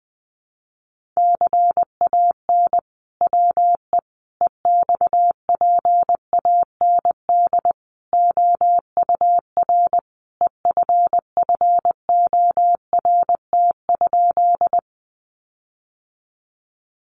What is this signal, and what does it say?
Morse code, heard as a keyed tone, about 20 wpm, 700 Hz. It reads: CAN WE EXPAND OUR EFFORT?
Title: Sentences from Top 1400 Words 20wpm